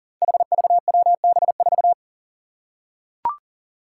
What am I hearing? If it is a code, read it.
Morse code, heard as a keyed tone, about 39 wpm, 700 Hz. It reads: HVWB4
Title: Five Challenging Letter-Number Combinations 40wpm